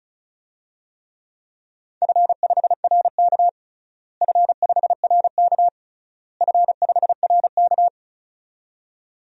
Morse code, heard as a keyed tone, about 36 wpm, 700 Hz. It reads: F5RK F5RK F5RK
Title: DXCC-100 Call Signs - 4 Characters - Repeated 3x - 35wpm